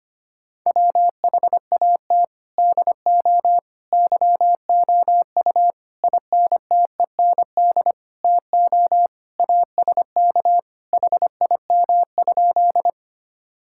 Morse code, heard as a keyed tone, about 25 wpm, 700 Hz. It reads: WHAT DO YOU INTEND TO ASK HIM?
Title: Sentences from Top 1600 Words 25wpm